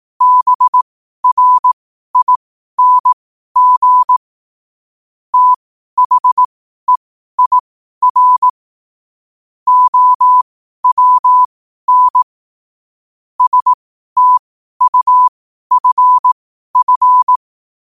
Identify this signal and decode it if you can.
Morse code, heard as a keyed tone, about 18 wpm, spaced as for 12 wpm, 1000 Hz. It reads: BRING THEIR OWN STUFF
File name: qotdfiles / 12wpm